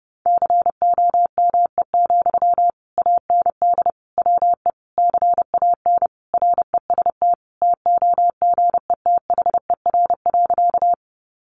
Morse code, heard as a keyed tone, about 30 wpm, 700 Hz. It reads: COME, AND WE CAN REST TOGETHER.
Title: Sentences from Top 500 Words 30wpm